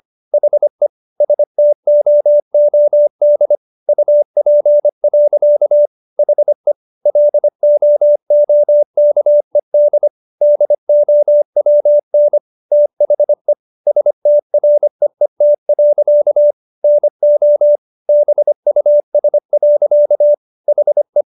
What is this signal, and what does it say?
Morse code, heard as a keyed tone, about 25 wpm, 590 Hz. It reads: HE STOOD UP. HE LOOKED DOWN THE STREET. NO BUS. HE